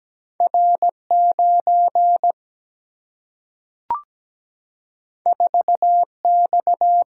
Morse code, heard as a keyed tone, about 17 wpm, 700 Hz. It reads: R 9 4 X
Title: Single-Letter Single-Number 17wpm